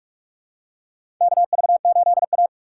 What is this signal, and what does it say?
Morse code, heard as a keyed tone, about 46 wpm, 700 Hz. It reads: KV8A
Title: USA Call Signs - 4 Characters - 3x Character Spacing - 45wpm